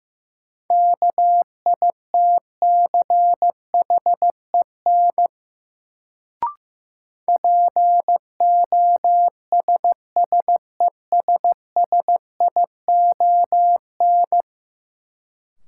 Morse code, heard as a keyed tone, about 15 wpm, 700 Hz. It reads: KITCHEN POSSESSION
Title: Sentences from Top 1400 Words - New Word Drill - 15wpm